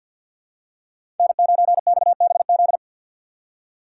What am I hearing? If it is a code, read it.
Morse code, heard as a keyed tone, about 52 wpm, 700 Hz. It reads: N9XBZ
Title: USA Call Signs - 4x Character Spacing - 50wpm